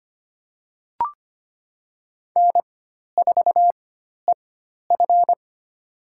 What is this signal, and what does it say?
Morse code, heard as a keyed tone, about 25 wpm, spaced as for 12 wpm, 700 Hz. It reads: N4EF
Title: USA Call Signs - 4 Characters - 4x Character Spacing - 25wpm